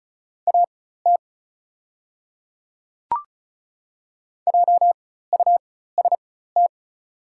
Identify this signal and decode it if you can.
Morse code, heard as a keyed tone, about 36 wpm, spaced as for 16 wpm, 700 Hz. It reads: AT JUST
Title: 4x Character Spacing - Top 200 Words - 35wpm